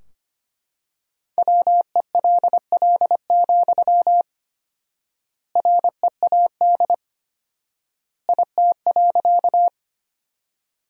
Morse code, heard as a keyed tone, about 25 wpm, 700 Hz. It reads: WELL, READ IT.
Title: 4x Word-Spacing - Sentences from Top 200 Words - 25wpm